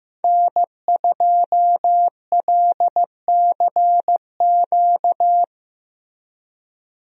Morse code, heard as a keyed tone, about 16 wpm, 700 Hz. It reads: N2LCQ
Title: USA Call Signs - 5 Characters - Repeated 2x - 15wpm